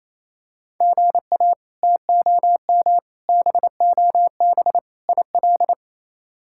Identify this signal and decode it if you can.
Morse code, heard as a keyed tone, about 28 wpm, 700 Hz. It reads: GA TOM BOB IL
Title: K1USN SST QSO Practice - 28wpm